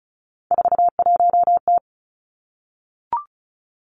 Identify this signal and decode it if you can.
Morse code, heard as a keyed tone, about 36 wpm, 700 Hz. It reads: 41T
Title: Three Challenging Letter-Number Combinations 35wpm